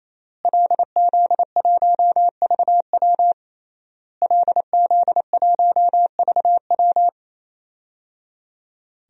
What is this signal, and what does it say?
Morse code, heard as a keyed tone, about 27 wpm, 700 Hz. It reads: LZ1VW LZ1VW
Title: DXCC-100 Call Signs - 5 Characters - Repeated 3x - 28wpm